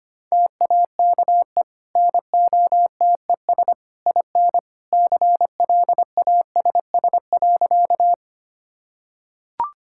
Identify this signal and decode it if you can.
Morse code, heard as a keyed tone, about 25 wpm, 700 Hz. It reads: TAKE NOTES IN CLASS.
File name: Easy Sentences with Hints - 25wpm